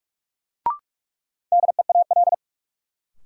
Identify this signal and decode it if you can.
Morse code, heard as a keyed tone, about 46 wpm, 700 Hz. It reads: DEAL
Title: Sets of 3 Words - New Word Drill - 45wpm